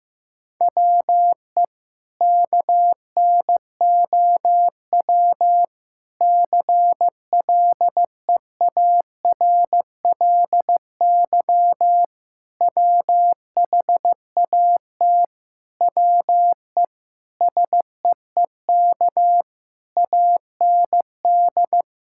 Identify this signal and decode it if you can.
Morse code, heard as a keyed tone, about 15 wpm, 700 Hz. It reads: WE KNOW CLEARLY WHAT WE SEEK AND
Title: Sentences from Top 1600 Words 15wpm